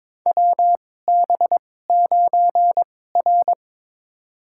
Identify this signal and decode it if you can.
Morse code, heard as a keyed tone, about 23 wpm, spaced as for 16 wpm, 700 Hz. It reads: WB9R
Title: USA Call Signs - 4 Characters - 2x Character Spacing - 22wpm